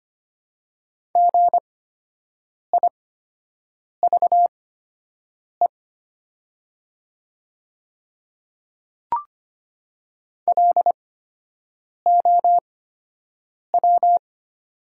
Morse code, heard as a keyed tone, about 25 wpm, spaced as for 7 wpm, 700 Hz. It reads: GIVE LOW
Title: 8x Character Spacing - Top 200 Words - 25wpm